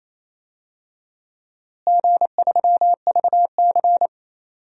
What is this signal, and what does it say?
Morse code, heard as a keyed tone, about 28 wpm, 700 Hz. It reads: G3VC
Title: DXCC-100 Call Signs - 4 Characters - Repeated 3x - 28wpm